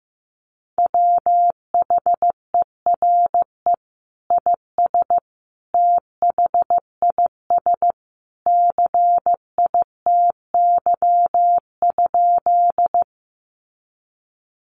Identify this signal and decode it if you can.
Morse code, heard as a keyed tone, about 15 wpm, 700 Hz. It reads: WHERE IS THIS CITY?
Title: Sentences from Top 300 Words 15wpm